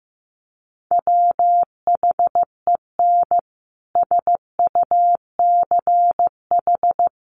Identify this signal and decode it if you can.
Morse code, heard as a keyed tone, about 15 wpm, 700 Hz. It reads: WHEN SUCH
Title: Sentences from Top 1400 Words 15wpm